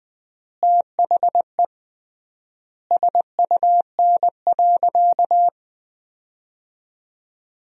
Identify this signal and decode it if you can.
Morse code, heard as a keyed tone, about 20 wpm, 700 Hz. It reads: THE SUN.
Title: 3x Word-Spacing - Sentences from Top 500 Words - 20wpm